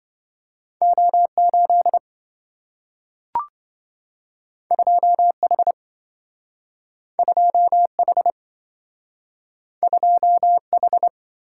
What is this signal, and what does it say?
Morse code, heard as a keyed tone, about 27 wpm, 700 Hz. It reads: O8 2H 2H 2H